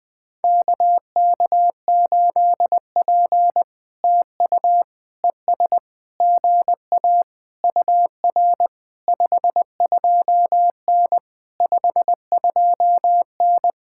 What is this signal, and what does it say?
Morse code, heard as a keyed tone, about 20 wpm, 700 Hz. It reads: KK8P TU ES GA UR 52N 52N
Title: POTA QSO Practice - 20wpm